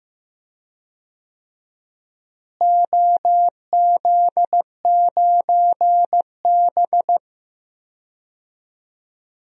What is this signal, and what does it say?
Morse code, heard as a keyed tone, about 16 wpm, 700 Hz. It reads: OZ9B
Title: DXCC-100 Call Signs - 4 Characters - Repeated 2x - 15wpm